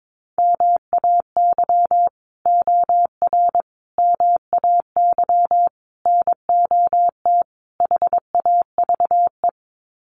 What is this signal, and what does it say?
Morse code, heard as a keyed tone, about 22 wpm, 700 Hz. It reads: MAY OR MAY NOT HAVE
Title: Sets of 5 Words Encore2 22wpm